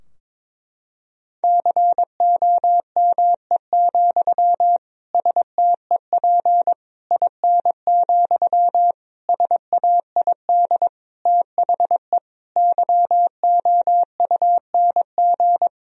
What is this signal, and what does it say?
Morse code, heard as a keyed tone, about 22 wpm, 700 Hz. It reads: COME, STEP IN, SAID THE YOUNG
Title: Sentences from Top 1000 Words - Review Part 2 - 22wpm